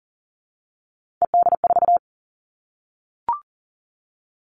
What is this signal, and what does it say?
Morse code, heard as a keyed tone, about 40 wpm, 700 Hz. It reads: ED4
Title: All Cut Numbers - 3-Digit 40wpm